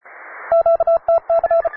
Morse code, heard as a keyed tone, about 34 wpm, 670 Hz. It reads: QTC